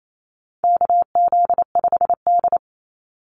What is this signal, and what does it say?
Morse code, heard as a keyed tone, about 28 wpm, 700 Hz. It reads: KZ5D
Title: Call Signs Repeated 2x - 28wpm